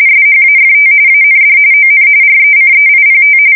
RTTY, about 45.45 baud. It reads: SHAFT AND ON RECEIPT
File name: fsk170 / morkrum